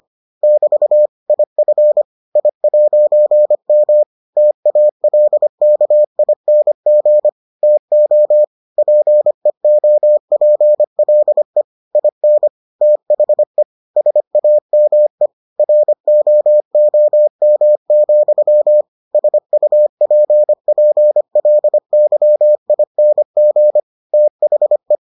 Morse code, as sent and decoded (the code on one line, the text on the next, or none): -...- / .. ..-. / .. .----. -- / - .- .-.. -.- .. -. --. / - --- / .--. --- .--. .-.. / .. -. / - .... / ... .- -- . / .-. --- --- -- --..-- / ... ..- .--. .--. .-.. -.-- .. -. --. / - .... .
<BT> IF I'M TALKING TO PEOPLE IN THE SAME ROOM, SUPPLYING THE